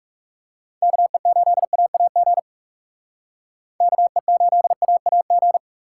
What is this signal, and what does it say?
Morse code, heard as a keyed tone, about 36 wpm, 700 Hz. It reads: KE8AAG KE8AAG
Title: Speed Racing - USA Call Signs 30wpm